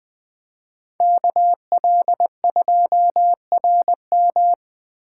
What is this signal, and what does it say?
Morse code, heard as a keyed tone, about 21 wpm, 700 Hz. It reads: KL2RM